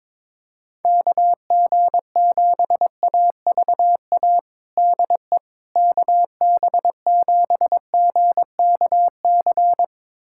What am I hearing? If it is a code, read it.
Morse code, heard as a keyed tone, about 22 wpm, 700 Hz. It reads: KG7AVA DE KB7GKC
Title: USA Callsign DE Callsign - 6 Characters - 22wpm